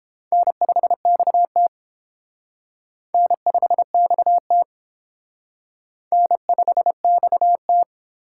Morse code, heard as a keyed tone, about 29 wpm, 700 Hz. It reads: N5XT N5XT N5XT